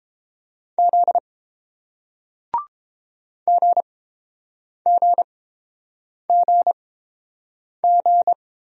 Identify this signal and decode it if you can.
Morse code, heard as a keyed tone, about 30 wpm, spaced as for 7 wpm, 700 Hz. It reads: Z GGGG